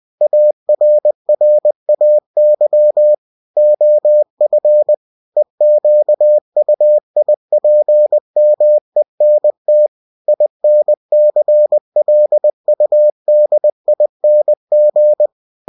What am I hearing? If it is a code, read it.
Morse code, heard as a keyed tone, about 20 wpm, 600 Hz. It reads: ARRAY OF EQUIPMENT INCLUDING
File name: cw-books / winnie-the-pooh / winnie-20wpm-page-0121